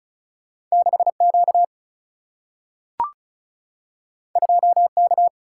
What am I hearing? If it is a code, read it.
Morse code, heard as a keyed tone, about 36 wpm, 700 Hz. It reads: 6Q 2K
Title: Single-Letter Single-Number 35wpm